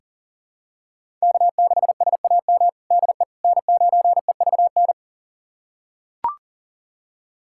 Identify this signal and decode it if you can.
Morse code, heard as a keyed tone, about 40 wpm, 700 Hz. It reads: K6SAM DE N9EVN
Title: USA Callsign DE Callsign - 5 Characters - 40wpm